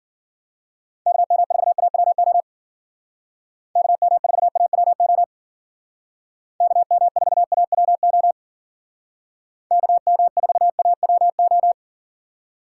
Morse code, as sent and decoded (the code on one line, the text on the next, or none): -.- -- ....- .- .-- --- / -.- -- ....- .- .-- --- / -.- -- ....- .- .-- --- / -.- -- ....- .- .-- ---
KM4AWO KM4AWO KM4AWO KM4AWO